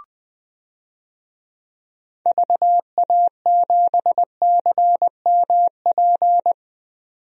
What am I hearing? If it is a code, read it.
Morse code, heard as a keyed tone, about 20 wpm, 700 Hz. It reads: VA7CMP
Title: DXCC-100 Call Signs - 6 Characters - 20wpm